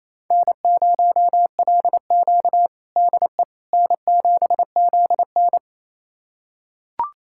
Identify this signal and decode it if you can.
Morse code, heard as a keyed tone, about 28 wpm, 700 Hz. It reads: N0LQ DE N7ZN
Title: USA Callsign DE Callsign - 4 Characters - 28wpm